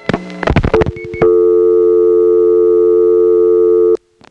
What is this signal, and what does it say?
Dial tone (North America), 350 + 440 Hz.